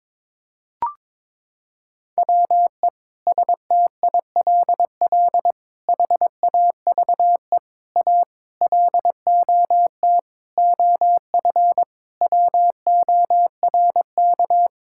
Morse code, heard as a keyed tone, about 22 wpm, 700 Hz. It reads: WE STILL HAVE A LOT OF WORK
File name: practice / 22wpm / Sentences with 8-Word Phrases - 22wpm